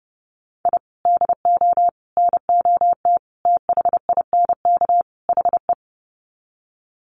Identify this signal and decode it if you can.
Morse code, heard as a keyed tone, about 30 wpm, 700 Hz. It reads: I DO NOT THINK HE
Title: Sets of 5 Words 30wpm